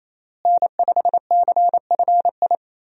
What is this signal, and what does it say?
Morse code, heard as a keyed tone, about 28 wpm, 700 Hz. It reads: N5CFI
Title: USA Call Signs - 5 Characters - Repeated 3x - 28wpm